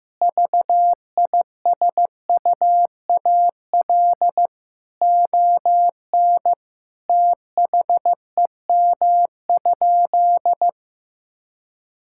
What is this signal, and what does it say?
Morse code, heard as a keyed tone, about 15 wpm, 700 Hz. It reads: VISUAL ON THEM?
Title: Sentences from Top 1000 Words - Review Part 6 - 15wpm